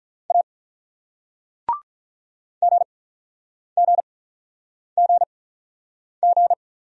Morse code, heard as a keyed tone, about 45 wpm, spaced as for 9 wpm, 700 Hz. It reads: A GGGG